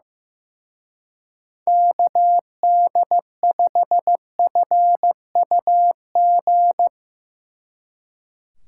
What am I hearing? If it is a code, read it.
Morse code, heard as a keyed tone, about 15 wpm, 700 Hz. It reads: KD5FUG
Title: Speed Racing - USA Call Signs 15wpm